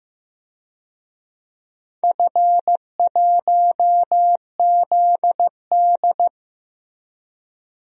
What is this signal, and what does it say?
Morse code, heard as a keyed tone, about 15 wpm, 700 Hz. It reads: F1ZD
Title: DXCC-100 Call Signs - 4 Characters - Repeated 2x - 15wpm